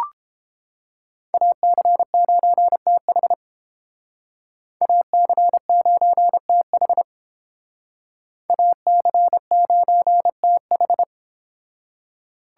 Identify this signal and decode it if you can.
Morse code, heard as a keyed tone, about 29 wpm, 700 Hz. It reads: AC9TH AC9TH AC9TH